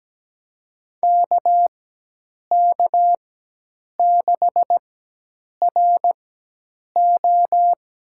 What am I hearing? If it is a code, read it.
Morse code, heard as a keyed tone, about 17 wpm, spaced as for 8 wpm, 700 Hz. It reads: KK6RO
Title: USA Call Signs - 5 Characters - 4x Character Spacing - 17wpm